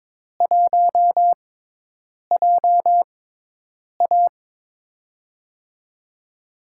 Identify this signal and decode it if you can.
Morse code, heard as a keyed tone, about 22 wpm, spaced as for 8 wpm, 700 Hz. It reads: 1JA
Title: USA Call Signs - 4 Characters - 6x Character Spacing - 22wpm